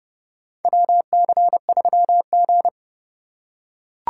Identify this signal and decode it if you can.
Morse code, heard as a keyed tone, about 32 wpm, 700 Hz. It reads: WC3G
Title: USA Call Signs - 4 Characters - 3x Character Spacing - 30wpm